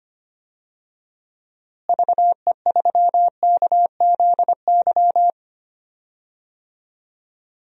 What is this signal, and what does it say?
Morse code, heard as a keyed tone, about 25 wpm, 700 Hz. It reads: VE3KZY
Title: DXCC-100 Call Signs - 6 Characters - Repeated 2x - 25wpm